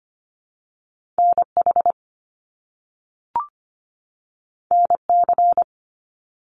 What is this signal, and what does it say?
Morse code, heard as a keyed tone, about 25 wpm, 700 Hz. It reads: NH NC
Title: US States 25wpm